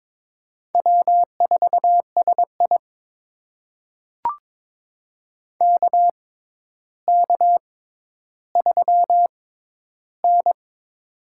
Morse code, heard as a keyed tone, about 22 wpm, spaced as for 11 wpm, 700 Hz. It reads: W4SI KK3N
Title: USA Call Signs - 4 Characters - 6x Character Spacing - 22wpm